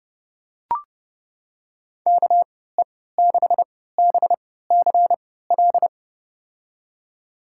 Morse code, heard as a keyed tone, about 30 wpm, spaced as for 17 wpm, 700 Hz. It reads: KE6BCL
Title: USA Call Signs - 6 Characters - 3x Character Spacing - 30wpm